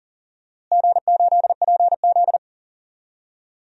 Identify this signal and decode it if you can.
Morse code, heard as a keyed tone, about 40 wpm, 700 Hz. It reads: G8PZ
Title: DXCC-100 Call Signs - 4 Characters - Repeated 2x - 40wpm